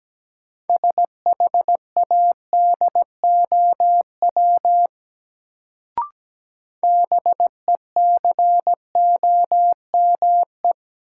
Morse code, heard as a keyed tone, about 17 wpm, 700 Hz. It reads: SHADOW BECOME
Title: Sets of 5 Words Encore - New Word Drill - 17wpm